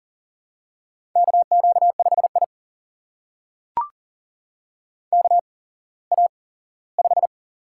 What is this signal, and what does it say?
Morse code, heard as a keyed tone, about 42 wpm, spaced as for 20 wpm, 700 Hz. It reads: KQ5I KA5
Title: USA Call Signs - 4 Characters - 8x Character Spacing - 40wpm